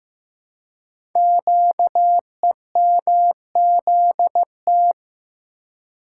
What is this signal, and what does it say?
Morse code, heard as a keyed tone, about 15 wpm, 700 Hz. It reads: QEMZT
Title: Code Groups - 5 Letters - 15wpm